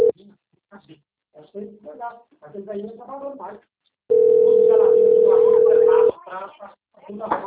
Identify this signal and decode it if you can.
Ringback tone (North America), 440 + 480 Hz, with speech.
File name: out- - - - - .5515